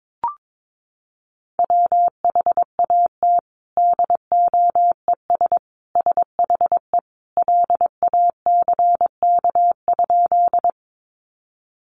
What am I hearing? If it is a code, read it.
Morse code, heard as a keyed tone, about 22 wpm, 700 Hz. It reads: WHAT DOES SHE LACK?
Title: Sentences from Top 1100 Words 22wpm